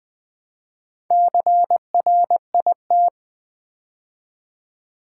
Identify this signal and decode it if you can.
Morse code, heard as a keyed tone, about 20 wpm, 700 Hz. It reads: CRIT
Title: Build A Word - 2 Letters At A Time - 8-Letter Words - 20wpm